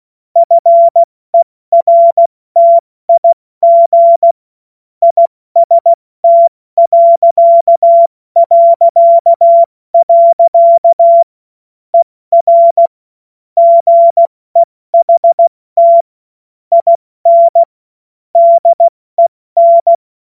Morse code, heard as a keyed tone, about 16 wpm, spaced as for 14 wpm, 680 Hz. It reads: FERTIG IST... ER GEHT IN DEN
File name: Witz-16-September-2025-70-F80